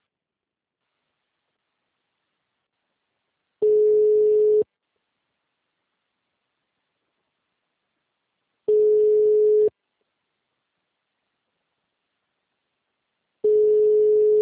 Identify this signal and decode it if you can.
Ringback tone (Europe, 425 Hz).
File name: out- -2002- - - .4559